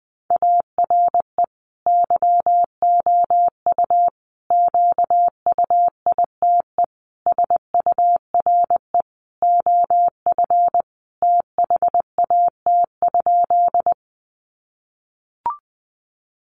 Morse code, heard as a keyed tone, about 20 wpm, 700 Hz. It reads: ARE YOU QUITE SURE OF THAT?
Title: Sentences from Top 700 Words 20wpm